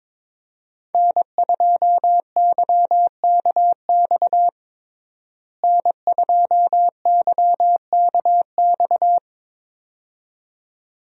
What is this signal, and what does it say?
Morse code, heard as a keyed tone, about 22 wpm, 700 Hz. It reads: N2YKX N2YKX
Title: USA Call Signs - 5 Characters - Repeated 2x - 22wpm